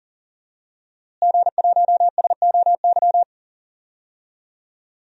Morse code, heard as a keyed tone, about 37 wpm, 700 Hz. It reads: G1SOY
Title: DXCC-100 Call Signs - 5 Characters - Repeated 2x - 40wpm